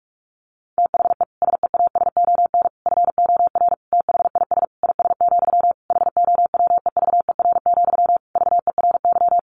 Morse code, heard as a keyed tone, about 44 wpm, 700 Hz. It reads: THE SEASON FOR THIS IS, HOWEVER, VERY
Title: Sentences from Top 1400 Words 45wpm